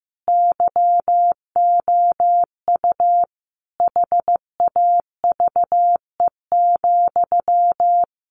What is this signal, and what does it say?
Morse code, heard as a keyed tone, about 15 wpm, 700 Hz. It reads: YOU HAVE,
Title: Sentences from Top 1100 Words 15wpm